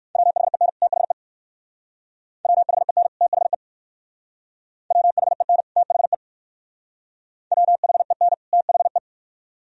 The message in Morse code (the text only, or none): WHEN THE WHEN THE WHEN THE WHEN THE